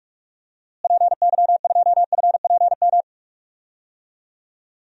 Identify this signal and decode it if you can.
Morse code, heard as a keyed tone, about 46 wpm, 700 Hz. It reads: PY2FPM
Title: DXCC-100 Call Signs - 6 Characters - Repeated 2x - 45wpm